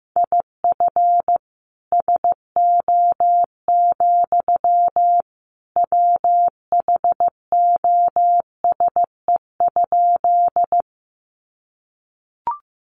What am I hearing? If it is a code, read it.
Morse code, heard as a keyed tone, about 15 wpm, 700 Hz. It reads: IF SO, WHOSE?
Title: Sentences from Top 1200 Words 15wpm